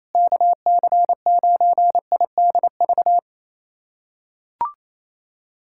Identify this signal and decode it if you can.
Morse code, heard as a keyed tone, about 28 wpm, 700 Hz. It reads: KC9IDV
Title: USA Callsign DE Callsign - 6 Characters - 28wpm